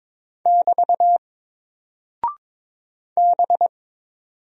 Morse code, heard as a keyed tone, about 21 wpm, 700 Hz. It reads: <BT> B
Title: Single Character (Letter:Number:Punctuation:Prosign) Alt 22wpm